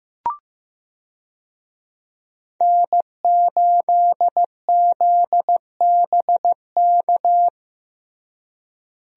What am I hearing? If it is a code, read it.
Morse code, heard as a keyed tone, about 15 wpm, 700 Hz. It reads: N8ZBK
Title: USA Call Signs - 5 Characters - Repeated 3x - 15wpm